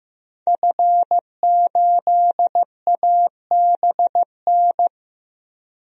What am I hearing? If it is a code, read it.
Morse code, heard as a keyed tone, about 15 wpm, 700 Hz. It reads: F8ABN
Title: DXCC-100 Call Signs - 5 Characters - 15wpm